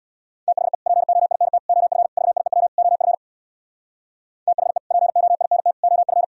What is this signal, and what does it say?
Morse code, heard as a keyed tone, about 64 wpm, 700 Hz. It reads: THE POINT OF VIEW OF THE POINT OF